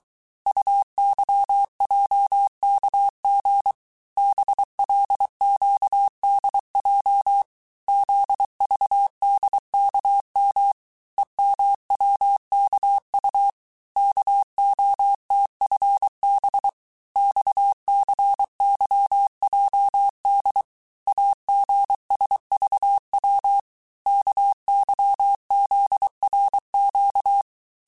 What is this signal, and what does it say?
Morse code, heard as a keyed tone, about 23 wpm, 800 Hz. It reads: UYJKG BLQDJ ZVDKM EMWKU KOTFB XCYJD AGSVW KYZRQ
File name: cours-du-mercredi / audios / cours-855-texte-chiffre-vitesse-22-mpm